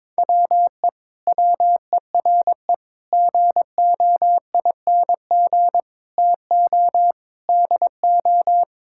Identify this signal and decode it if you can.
Morse code, heard as a keyed tone, about 22 wpm, 700 Hz. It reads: WE WERE GOING TO DO
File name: practice / 22wpm / Sentences with 8-Word Phrases - 22wpm